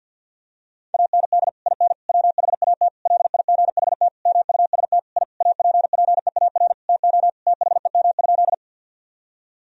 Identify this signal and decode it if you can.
Morse code, heard as a keyed tone, about 50 wpm, 700 Hz. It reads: AND IN WHAT LIGHT MUST I APPEAR TO THEM?
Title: Sentences from Top 1000 Words - Review Part 2 - 50wpm